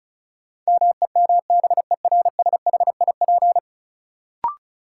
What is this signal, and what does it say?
Morse code, heard as a keyed tone, about 36 wpm, 700 Hz. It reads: MEMBERSHIP
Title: Sentences from Top 1400 Words - New Word Drill Alt - 35wpm